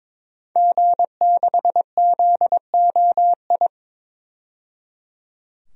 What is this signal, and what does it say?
Morse code, heard as a keyed tone, about 22 wpm, 700 Hz. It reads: G6ZOI
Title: DXCC-100 Call Signs - 5 Characters - 22wpm